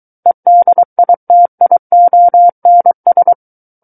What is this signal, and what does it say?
Morse code, heard as a keyed tone, about 23 wpm, 700 Hz. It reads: EDITIONS
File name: Book One / 14-Chapter 14 - 23 WPM